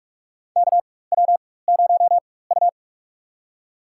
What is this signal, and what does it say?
Morse code, heard as a keyed tone, about 47 wpm, spaced as for 21 wpm, 700 Hz. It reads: KW0U
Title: USA Call Signs - 4 Characters - 4x Character Spacing - 45wpm